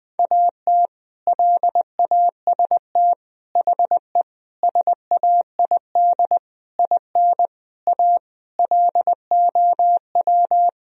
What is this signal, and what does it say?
Morse code, heard as a keyed tone, about 21 wpm, 700 Hz. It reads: AT LAST HE SAID IN A LOW
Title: Sentences from Top 1000 Words - Review Part 1 - 20wpm